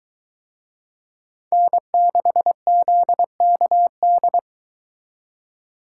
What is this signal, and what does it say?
Morse code, heard as a keyed tone, about 23 wpm, 700 Hz. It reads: N6ZKD